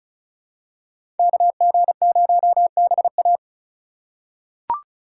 Morse code, heard as a keyed tone, about 36 wpm, 700 Hz. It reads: KG0BA
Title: USA Call Signs - 5 Characters - 4x Character Spacing - 35wpm